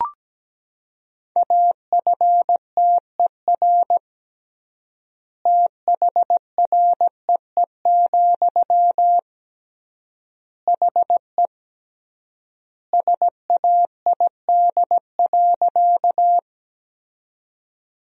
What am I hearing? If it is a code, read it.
Morse code, heard as a keyed tone, about 17 wpm, 700 Hz. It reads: AFTER THREE, HE SAID.
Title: 3x Word-Spacing - Sentences from Top 200 Words - 17wpm